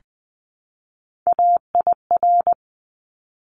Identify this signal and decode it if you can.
Morse code, heard as a keyed tone, about 20 wpm, 700 Hz. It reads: AIR
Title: Top 200 Words Repeated 20wpm